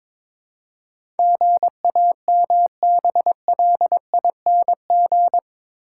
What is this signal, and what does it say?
Morse code, heard as a keyed tone, about 22 wpm, 700 Hz. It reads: GAMBLING
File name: practice / 22wpm / Eight Letter Words - 22wpm